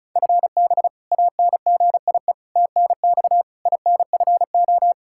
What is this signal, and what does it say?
Morse code, heard as a keyed tone, about 34 wpm, 700 Hz. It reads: FB ANGIE TNX INFO
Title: K1USN SST QSO Practice 35wpm